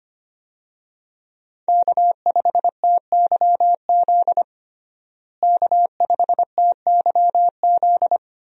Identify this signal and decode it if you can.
Morse code, heard as a keyed tone, about 25 wpm, 700 Hz. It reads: K5TYZ K5TYZ
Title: USA Call Signs - 5 Characters - Repeated 3x - 25wpm